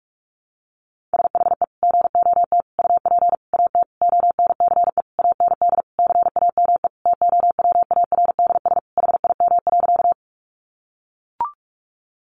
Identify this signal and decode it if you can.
Morse code, heard as a keyed tone, about 44 wpm, 700 Hz. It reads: SHE GOT UP AT ONCE AND CAME TOWARDS HIM.
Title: Sentences from Top 1100 Words 45wpm